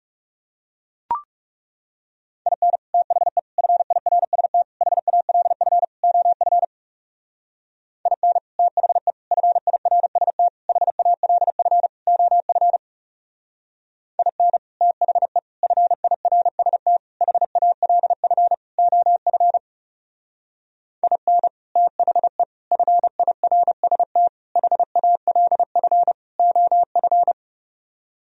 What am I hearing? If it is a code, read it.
Morse code, heard as a keyed tone, about 35 wpm, 700 Hz. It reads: IN THE FIRST HALF OF IN THE FIRST HALF OF IN THE FIRST HALF OF IN THE FIRST HALF OF
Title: Speed-Racing - Sets of 5 Words 30wpm